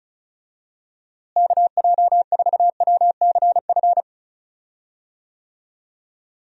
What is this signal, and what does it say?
Morse code, heard as a keyed tone, about 34 wpm, 700 Hz. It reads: KJ4WCF
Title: USA Call Signs - 6 Characters - Repeated 3x - 35wpm